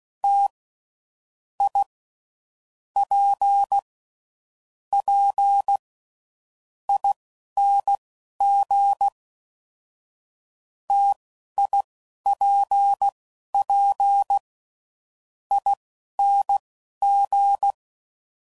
Morse code, heard as a keyed tone, about 16 wpm, spaced as for 12 wpm, 800 Hz. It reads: T I P P ING TIPP ING
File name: Mixed / Variable Sp / T AOEIs